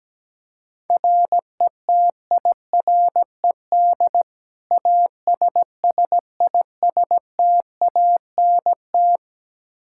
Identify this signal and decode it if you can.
Morse code, heard as a keyed tone, about 17 wpm, 700 Hz. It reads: RETIRED ASSISTANT
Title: Practice QSO - Exchange 3 - More Info - 17wpm